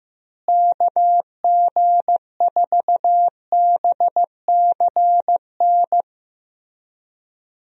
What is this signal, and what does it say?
Morse code, heard as a keyed tone, about 15 wpm, 700 Hz. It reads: KG4BCN
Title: USA Call Signs - 6 Characters - Repeated 3x - 15wpm